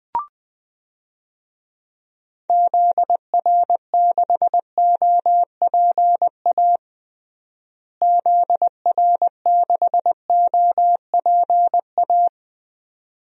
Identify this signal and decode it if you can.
Morse code, heard as a keyed tone, about 21 wpm, 700 Hz. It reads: ZR6OPA ZR6OPA
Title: DXCC-100 Call Signs - 6 Characters - Repeated 2x - 20wpm